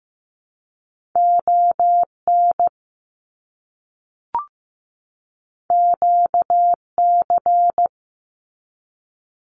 Morse code, heard as a keyed tone, about 15 wpm, 700 Hz. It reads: ON QC
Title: Canadian Provinces and Territories 15wpm